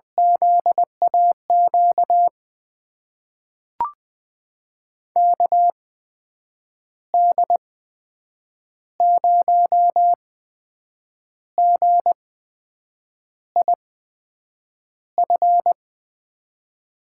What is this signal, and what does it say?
Morse code, heard as a keyed tone, about 20 wpm, spaced as for 7 wpm, 700 Hz. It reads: ZAQ KD0GIF
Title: USA Call Signs - 6 Characters - 8x Character Spacing - 20wpm